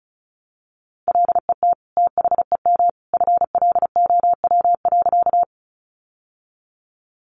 Morse code, heard as a keyed tone, about 34 wpm, 700 Hz. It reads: LET THEM FLOW.
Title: Sentences from Top 1200 Words 35wpm